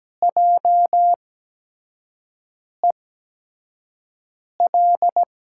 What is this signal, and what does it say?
Morse code, heard as a keyed tone, about 17 wpm, spaced as for 5 wpm, 700 Hz. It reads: JEL
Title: USA Call Signs - 6 Characters - 8x Character Spacing - 17wpm